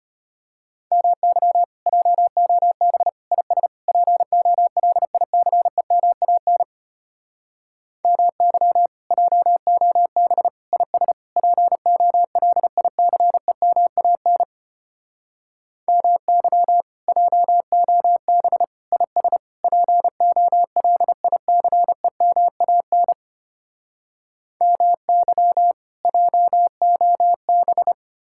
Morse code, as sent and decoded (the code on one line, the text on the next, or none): -- -.-- / .--- --- -... / .. ... / .--. --- .-.. .. -.-. -- .- -. / -- -.-- / .--- --- -... / .. ... / .--. --- .-.. .. -.-. -- .- -. / -- -.-- / .--- --- -... / .. ... / .--. --- .-.. .. -.-. -- .- -. / -- -.-- / .--- --- -...
MY JOB IS POLICEMAN MY JOB IS POLICEMAN MY JOB IS POLICEMAN MY JOB